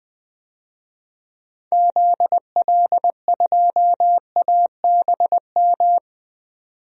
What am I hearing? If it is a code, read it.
Morse code, heard as a keyed tone, about 20 wpm, 700 Hz. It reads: ZL2ABM
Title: DXCC-100 Call Signs - 6 Characters - 20wpm